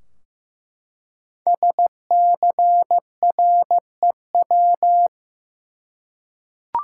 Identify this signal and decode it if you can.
Morse code, heard as a keyed tone, about 15 wpm, 700 Hz. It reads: SCREW
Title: Five Letter Words - 15wpm